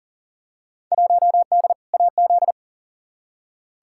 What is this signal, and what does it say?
Morse code, heard as a keyed tone, about 39 wpm, 700 Hz. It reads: 1D AZ
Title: Field Day Exchanges - 40wpm